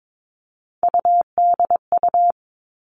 Morse code, heard as a keyed tone, about 22 wpm, 700 Hz. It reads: UDU
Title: Three Challenging Letter-Number Combinations 22wpm